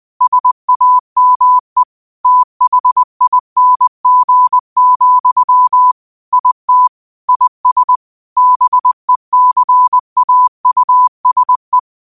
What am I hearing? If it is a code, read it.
Morse code, heard as a keyed tone, about 20 wpm, 1000 Hz. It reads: SAME THING, IT IS BECAUSE